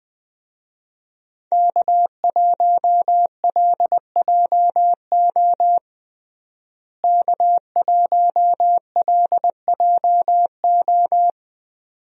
Morse code, heard as a keyed tone, about 20 wpm, 700 Hz. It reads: K1LJO K1LJO
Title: USA Call Signs - 5 Characters - Repeated 2x - 20wpm